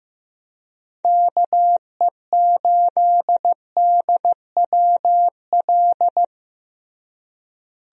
Morse code, heard as a keyed tone, about 16 wpm, 700 Hz. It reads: KE8DWL
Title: USA Call Signs - 6 Characters - Repeated 3x - 15wpm